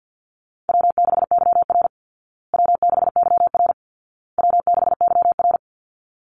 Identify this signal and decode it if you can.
Morse code, heard as a keyed tone, about 50 wpm, 700 Hz. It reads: W6YR W6YR W6YR
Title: Easy Call Signs Repeated 3x - 50wpm